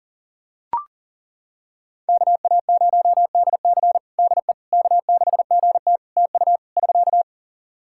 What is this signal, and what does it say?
Morse code, heard as a keyed tone, about 40 wpm, 700 Hz. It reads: KA0DC DE K6GT TU <SK>
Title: Practice QSO - Exchange 4 - Ending - 40wpm